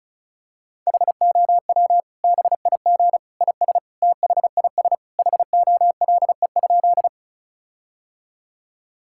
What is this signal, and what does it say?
Morse code, heard as a keyed tone, about 36 wpm, 700 Hz. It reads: HOW BIG IS THIS HOLE?
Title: Sentences from Top 1600 Words 35wpm